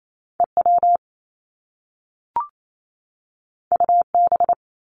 Morse code, heard as a keyed tone, about 28 wpm, 700 Hz. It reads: EW UB
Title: Top Two Letter Combinations in Words 28wpm